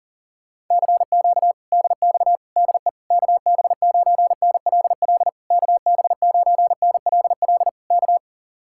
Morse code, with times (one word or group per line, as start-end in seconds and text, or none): 0.7-1.5 CQ
1.7-2.4 DX
2.6-2.9 DE
3.1-5.3 KB9NLL
5.5-7.7 KB9NLL
7.9-8.2 K